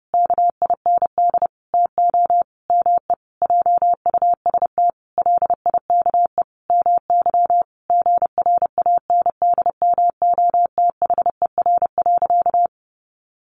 Morse code, heard as a keyed tone, about 30 wpm, 700 Hz. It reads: KIND TO ME JUST LIKE MY GRANDMOTHER.
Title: Sentences from Top 1200 Words 30wpm